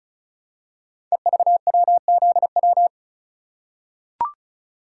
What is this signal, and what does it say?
Morse code, heard as a keyed tone, about 36 wpm, 700 Hz. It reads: EVWZW
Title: Code Groups - 5 Letter:Numbers - 35wpm